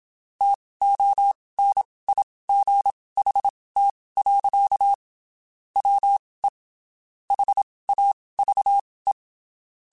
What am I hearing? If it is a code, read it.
Morse code, heard as a keyed tone, about 27 wpm, spaced as for 19 wpm, 800 Hz. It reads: TONIGHT. WE HAVE
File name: (spaced) / QSOs Spaced-mp3s / WD0BGP 1 @25 sp